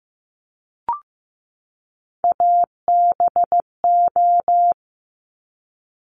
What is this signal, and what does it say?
Morse code, heard as a keyed tone, about 15 wpm, 700 Hz. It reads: ABO
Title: Top Three Letter Combinations in Words 15wpm